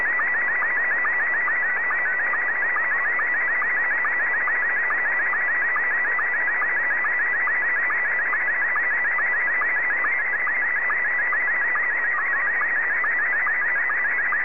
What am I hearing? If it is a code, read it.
SSTV (slow-scan television), Scottie 1.